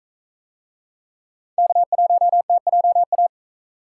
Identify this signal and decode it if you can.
Morse code, heard as a keyed tone, about 41 wpm, 700 Hz. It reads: K1TJA